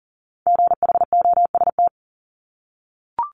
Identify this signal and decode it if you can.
Morse code, heard as a keyed tone, about 38 wpm, 690 Hz. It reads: GHOST
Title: Multi-letter Phonemes 40wpm